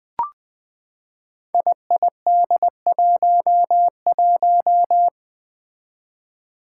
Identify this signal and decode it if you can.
Morse code, heard as a keyed tone, about 20 wpm, 700 Hz. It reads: IID11
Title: Five Challenging Letter-Number Combinations 20wpm